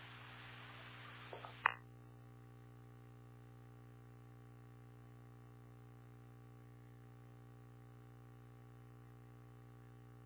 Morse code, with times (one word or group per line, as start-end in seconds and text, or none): none